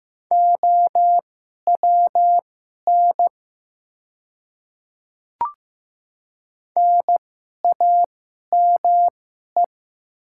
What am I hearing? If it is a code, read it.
Morse code, heard as a keyed tone, about 15 wpm, spaced as for 11 wpm, 700 Hz. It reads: OWN NAME
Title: 2x Character Spacing - Top 500 Words - 15wpm